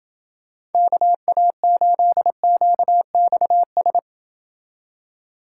Morse code, heard as a keyed tone, about 29 wpm, 700 Hz. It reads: KA8QXS